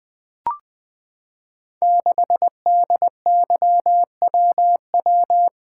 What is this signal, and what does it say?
Morse code, heard as a keyed tone, about 20 wpm, 700 Hz. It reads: 6DYWW
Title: Code Groups - 5 Letter:Numbers - 20wpm